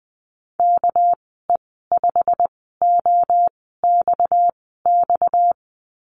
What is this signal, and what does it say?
Morse code, heard as a keyed tone, about 20 wpm, spaced as for 14 wpm, 700 Hz. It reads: KE5OXX
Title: 245 Lesson 39g - Letter z - Call Signs 20wpm